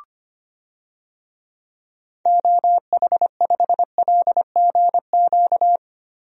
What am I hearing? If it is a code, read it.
Morse code, heard as a keyed tone, about 25 wpm, 700 Hz. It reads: OH5LGQ
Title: DXCC-100 Call Signs - 6 Characters - Repeated 2x - 25wpm